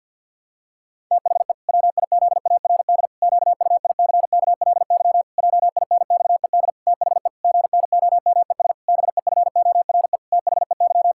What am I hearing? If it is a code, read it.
Morse code, heard as a keyed tone, about 50 wpm, 700 Hz. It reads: THE WIZARD QUICKLY JINXED THE GNOMES BEFORE THEY
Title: Pangram Challenge v2 - 50wpm